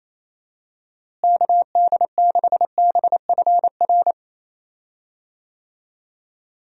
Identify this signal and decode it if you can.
Morse code, heard as a keyed tone, about 27 wpm, 700 Hz. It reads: KD6BFR
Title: USA Call Signs - 6 Characters - Repeated 2x - 28wpm